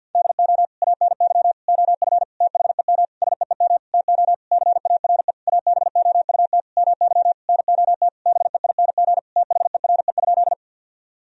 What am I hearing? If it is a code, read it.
Morse code, heard as a keyed tone, about 50 wpm, 700 Hz. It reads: DO ANY OF THEM SEEM TO CARE ABOUT MY NOT BEING THERE?
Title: Sentences from Top 1000 Words - Review Part 3 - 50wpm